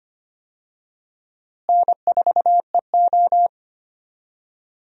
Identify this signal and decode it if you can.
Morse code, heard as a keyed tone, about 25 wpm, 700 Hz. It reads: N4EO
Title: USA Call Signs - 4 Characters - Repeated 2x - 25wpm